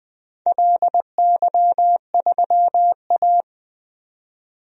Morse code, heard as a keyed tone, about 20 wpm, 700 Hz. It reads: LY3A
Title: DXCC-100 Call Signs - 4 Characters - 20wpm